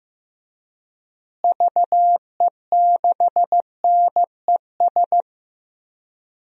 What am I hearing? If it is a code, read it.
Morse code, heard as a keyed tone, about 15 wpm, 700 Hz. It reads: VE6NES
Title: DXCC-100 Call Signs - 6 Characters - Repeated 3x - 15wpm